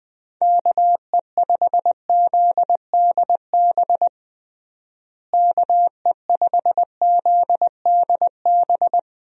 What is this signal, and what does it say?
Morse code, heard as a keyed tone, about 20 wpm, 700 Hz. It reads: KE5ZDB KE5ZDB
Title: USA Call Signs - 6 Characters - Repeated 2x - 20wpm